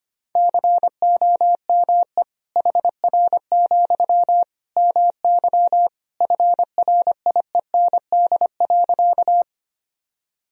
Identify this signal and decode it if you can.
Morse code, heard as a keyed tone, about 25 wpm, 700 Hz. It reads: COME HR, MY FRIEND.
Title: Sentences from Top 300 Words and Abbreviations 25wpm